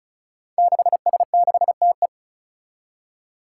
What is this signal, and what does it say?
Morse code, heard as a keyed tone, about 36 wpm, 700 Hz. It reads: 6S6TE
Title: Five Challenging Letter-Number Combinations 35wpm